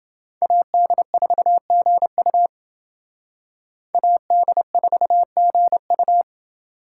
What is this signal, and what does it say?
Morse code, heard as a keyed tone, about 29 wpm, 700 Hz. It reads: AD4GU AD4GU